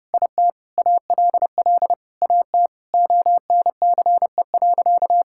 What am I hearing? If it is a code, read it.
Morse code, heard as a keyed tone, about 30 wpm, 700 Hz. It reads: IT ALL AT ONCE.
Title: Sentences from Top 1000 Words - Review Part 5 - 30wpm